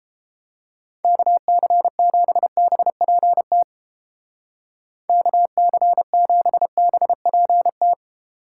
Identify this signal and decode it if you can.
Morse code, heard as a keyed tone, about 29 wpm, 700 Hz. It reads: KC7BPT KC7BPT